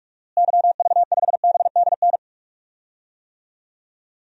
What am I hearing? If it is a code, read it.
Morse code, heard as a keyed tone, about 42 wpm, 700 Hz. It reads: YV5BDN
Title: DXCC-100 Call Signs - 6 Characters - Repeated 2x - 45wpm